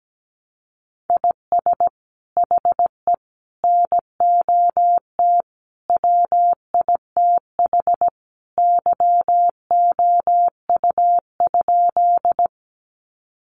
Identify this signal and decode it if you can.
Morse code, heard as a keyed tone, about 17 wpm, 700 Hz. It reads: IS HE NOT WITH YOU?
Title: Sentences from Top 600 Words 17wpm